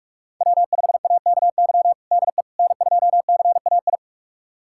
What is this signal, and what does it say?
Morse code, heard as a keyed tone, about 46 wpm, 700 Hz. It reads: W5AKY DE N2CAI
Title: USA Callsign DE Callsign - 5 Characters - 45wpm